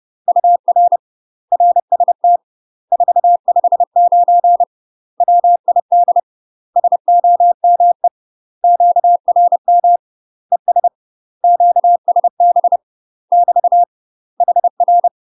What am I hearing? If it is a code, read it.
Morse code, heard as a keyed tone, about 30 wpm, 700 Hz. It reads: UR RST 459 WID SOME QRM ES QSB <BT> HR
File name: UK MASTER QSOs TRAINING / 30WPM / UK MASTER QSOs 0031